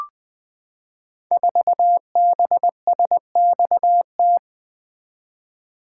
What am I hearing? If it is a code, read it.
Morse code, heard as a keyed tone, about 20 wpm, 700 Hz. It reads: 4BSXT
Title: Code Groups - 5 Letter:Numbers - 20wpm